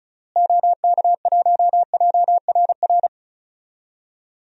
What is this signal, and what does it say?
Morse code, heard as a keyed tone, about 36 wpm, 700 Hz. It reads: OK1JRR
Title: DXCC-100 Call Signs - 6 Characters - Repeated 3x - 35wpm